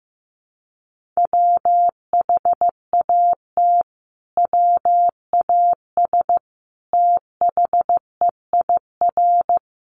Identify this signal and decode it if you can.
Morse code, heard as a keyed tone, about 15 wpm, 700 Hz. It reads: WHAT WAS THEIR
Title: Sentences from Top 1400 Words 15wpm